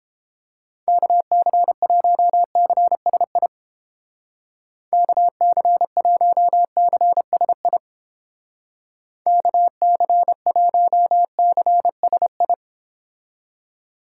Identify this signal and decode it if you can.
Morse code, heard as a keyed tone, about 29 wpm, 700 Hz. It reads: KC1CSI KC1CSI KC1CSI